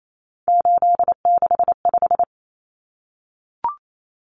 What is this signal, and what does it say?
Morse code, heard as a keyed tone, about 28 wpm, 700 Hz. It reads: 865
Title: Common Cut Numbers - 3-Digit 28wpm